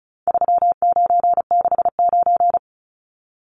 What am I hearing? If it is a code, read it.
Morse code, heard as a keyed tone, about 34 wpm, 700 Hz. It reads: 3969
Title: Numbers - 4-Digit 35wpm